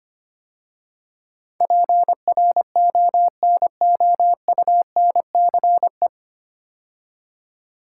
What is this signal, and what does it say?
Morse code, heard as a keyed tone, about 25 wpm, 700 Hz. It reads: PRONOUNCE
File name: practice / 25wpm / 9 Letter Words - First 3 Letters Given - 25wpm